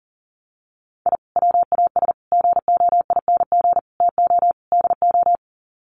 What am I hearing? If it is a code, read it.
Morse code, heard as a keyed tone, about 40 wpm, 700 Hz. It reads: I WAS GOING TO DO
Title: Sets of 5 Words Encore2 40wpm